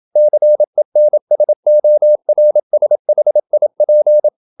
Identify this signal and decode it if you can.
Morse code, heard as a keyed tone, about 27 wpm, 600 Hz. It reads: CENSORSHIP
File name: Never 0027wpm